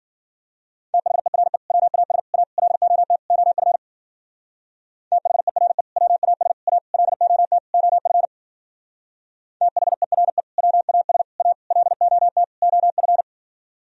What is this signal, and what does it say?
Morse code, heard as a keyed tone, about 50 wpm, 700 Hz. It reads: THERE WAS A LOT OF THERE WAS A LOT OF THERE WAS A LOT OF